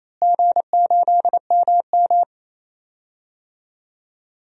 Morse code, heard as a keyed tone, about 28 wpm, 700 Hz. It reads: G8MM
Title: DXCC-100 Call Signs - 4 Characters - Repeated 3x - 28wpm